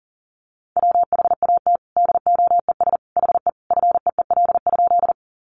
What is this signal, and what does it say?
Morse code, heard as a keyed tone, about 39 wpm, 700 Hz. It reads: WHAT DOES HE FEEL?
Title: Sentences from Top 400 Words 40wpm